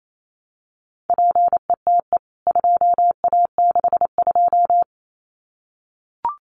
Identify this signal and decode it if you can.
Morse code, heard as a keyed tone, about 28 wpm, 700 Hz. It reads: PETE 2A62
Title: CWT Exchange 28wpm